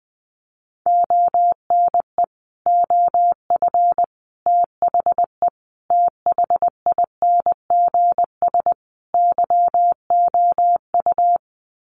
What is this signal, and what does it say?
Morse code, heard as a keyed tone, about 20 wpm, 700 Hz. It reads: ONE OF THE THINGS YOU
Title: Sets of 5 Words Encore2 20wpm